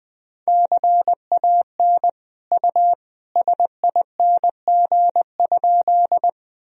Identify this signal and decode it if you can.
Morse code, heard as a keyed tone, about 20 wpm, 700 Hz. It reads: CAN U SING?
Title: Sentences from Top 400 Words and Abbreviations 20wpm